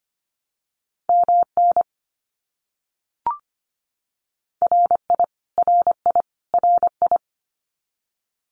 Morse code, heard as a keyed tone, about 25 wpm, 700 Hz. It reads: MN RI RI RI
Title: US States Repeated 3x 25wpm